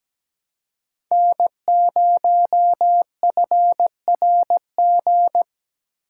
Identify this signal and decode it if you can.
Morse code, heard as a keyed tone, about 20 wpm, 700 Hz. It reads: N0FRG
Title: Speed Racing - USA 1x3 Call Signs 17wpm